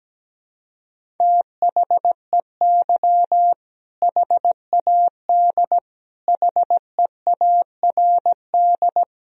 Morse code, heard as a keyed tone, about 17 wpm, 700 Hz. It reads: THEY HAD HEARD
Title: Sentences from Top 400 Words and Abbreviations 17wpm